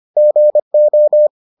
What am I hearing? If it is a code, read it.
Morse code, heard as a keyed tone, about 25 wpm, 600 Hz. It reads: GO